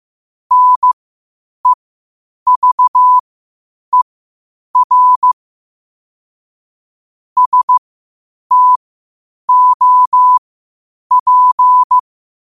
Morse code, heard as a keyed tone, about 15 wpm, spaced as for 8 wpm, 1000 Hz. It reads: NEVER STOP
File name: qotdfiles / 8wpm